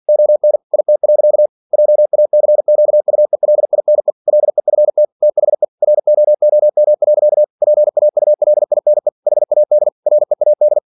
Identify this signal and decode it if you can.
Morse code, heard as a keyed tone, about 51 wpm, 600 Hz. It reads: ON IT. JACQUELINE LEFT THE ROOM. PAULINE HAD READ
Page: Never 47wpm